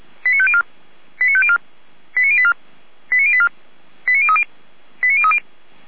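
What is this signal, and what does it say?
Five-tone selective call (ZVEI), heard as quick runs of five tones, tones of about 70 ms. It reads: 89504 89504 89084 89084 89030 89030
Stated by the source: ZVEI five-tone selective call 89042 received by SDR.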